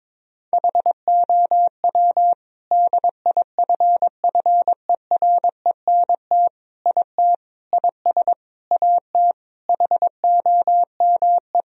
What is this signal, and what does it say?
Morse code, heard as a keyed tone, about 22 wpm, 700 Hz. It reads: HOW DIFFERENT IT IS AT HOME
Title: Sentences from Top 1000 Words - Review Part 3 - 22wpm